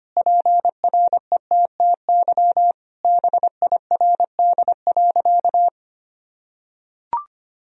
Morse code, heard as a keyed tone, about 25 wpm, 700 Hz. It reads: PRETTY BIRD.
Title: Sentences from Top 1000 Words - Review Part 2 - 25wpm